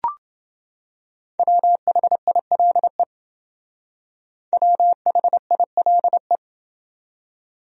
Morse code, heard as a keyed tone, about 29 wpm, 700 Hz. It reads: WHILE WHILE